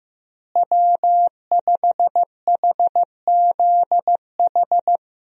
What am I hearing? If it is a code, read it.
Morse code, heard as a keyed tone, about 15 wpm, 700 Hz. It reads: W5HZH
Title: USA Call Signs - 5 Characters - Repeated 3x - 15wpm